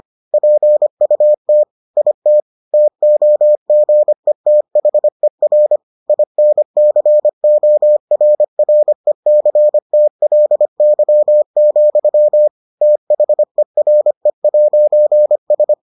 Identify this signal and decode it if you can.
Morse code, heard as a keyed tone, about 25 wpm, 600 Hz. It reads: PUT IT TOGETHER INCORRECTLY, THERE'S